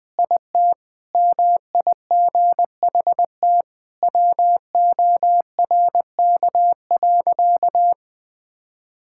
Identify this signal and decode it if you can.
Morse code, heard as a keyed tone, about 20 wpm, 700 Hz. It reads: IT MIGHT WORK.
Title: Sentences with 3-Word Phrases - Spoken Beginning - 20wpm